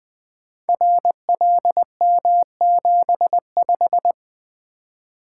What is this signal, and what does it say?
Morse code, heard as a keyed tone, about 20 wpm, 700 Hz. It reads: RLM75
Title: Code Groups - 5 Letter:Numbers - 20wpm